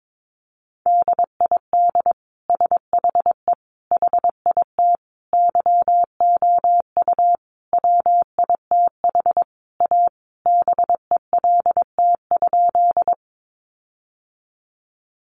Morse code, heard as a keyed tone, about 22 wpm, 700 Hz. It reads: DID SHE HIT YOU WITH A BELT?
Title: Sentences from Top 1400 Words 22wpm